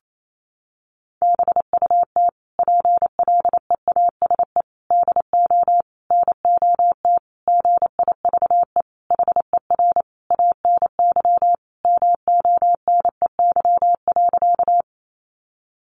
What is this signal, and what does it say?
Morse code, heard as a keyed tone, about 28 wpm, 700 Hz. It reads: BUT PLEASE DO NOT GIVE HER ANY MONEY.
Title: Sentences from Top 1500 Words 28wpm